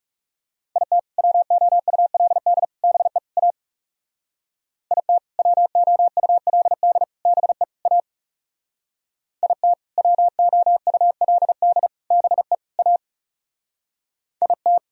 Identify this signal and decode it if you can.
Morse code, heard as a keyed tone, about 40 wpm, 700 Hz. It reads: IT WOULD BE A IT WOULD BE A IT WOULD BE A IT